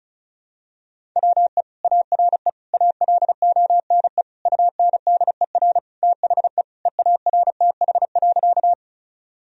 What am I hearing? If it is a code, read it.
Morse code, heard as a keyed tone, about 36 wpm, 700 Hz. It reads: WE ARE ALONE UNDER THE EARTH.
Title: Sentences from Top 1000 Words - Review Part 1 - 35wpm